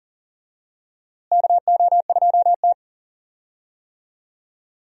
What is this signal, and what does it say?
Morse code, heard as a keyed tone, about 40 wpm, 700 Hz. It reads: KO2T
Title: USA Call Signs - 4 Characters - Repeated 3x - 40wpm